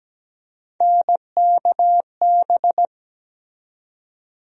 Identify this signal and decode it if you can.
Morse code, heard as a keyed tone, about 17 wpm, 700 Hz. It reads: NKB
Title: USA Call Sign Postfixes - 3 Letters 17wpm